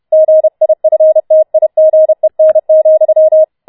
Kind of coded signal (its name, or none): Morse code